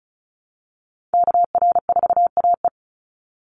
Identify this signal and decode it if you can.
Morse code, heard as a keyed tone, about 34 wpm, 700 Hz. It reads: KR4AE
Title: Call Signs Repeated 2x - 35wpm